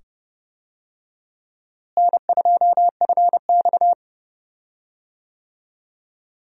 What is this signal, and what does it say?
Morse code, heard as a keyed tone, about 30 wpm, 700 Hz. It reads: N2FX
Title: USA Call Signs - 4 Characters - Repeated 2x - 30wpm